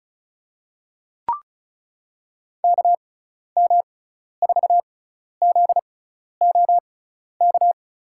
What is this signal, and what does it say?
Morse code, heard as a keyed tone, about 36 wpm, spaced as for 12 wpm, 700 Hz. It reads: KM4ZOK
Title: USA Call Signs - 6 Characters - 6x Character Spacing - 35wpm